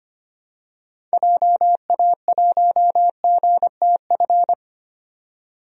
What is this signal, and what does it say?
Morse code, heard as a keyed tone, about 25 wpm, 700 Hz. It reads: JA1GTF
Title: DXCC-100 Call Signs - 6 Characters - 25wpm